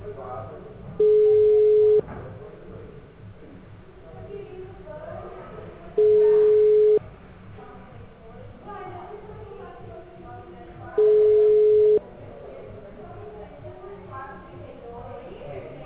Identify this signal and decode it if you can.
Ringback tone (Europe, 425 Hz), with speech.